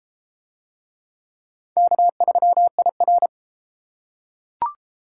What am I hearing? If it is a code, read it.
Morse code, heard as a keyed tone, about 31 wpm, 700 Hz. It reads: K3IR